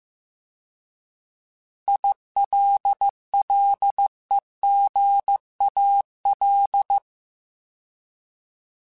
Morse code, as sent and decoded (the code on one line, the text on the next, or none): .. .-.. .-.. --. .- .-..
ILLEGAL